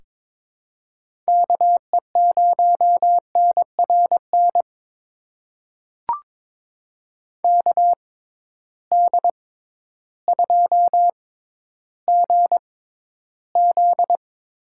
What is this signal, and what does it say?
Morse code, heard as a keyed tone, about 22 wpm, spaced as for 12 wpm, 700 Hz. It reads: KE0NRN KD2GZ
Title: USA Call Signs - 6x Character Spacing - 22wpm